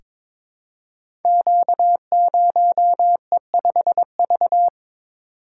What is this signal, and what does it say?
Morse code, heard as a keyed tone, about 22 wpm, 700 Hz. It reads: Q0E5V
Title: Code Groups - 5 Letter:Numbers - 22wpm